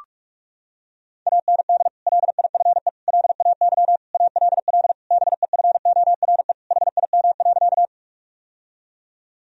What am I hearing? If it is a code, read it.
Morse code, heard as a keyed tone, about 46 wpm, 700 Hz. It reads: AND LIFE LAY ALL BEFORE HIM.
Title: Sentences from Top 1000 Words - Review Part 2 - 45wpm